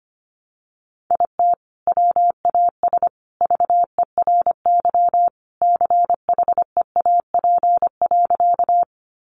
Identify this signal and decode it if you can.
Morse code, heard as a keyed tone, about 25 wpm, 700 Hz. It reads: IT WAS VERY CHEAP.
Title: Sentences from Top 1200 Words 25wpm